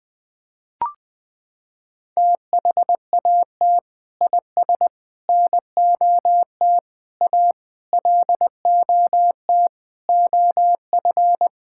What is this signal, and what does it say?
Morse code, heard as a keyed tone, about 20 wpm, 700 Hz. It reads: THAT IS NOT A LOT OF
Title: Sentences from Top 1000 Words - Review Part 5 - 20wpm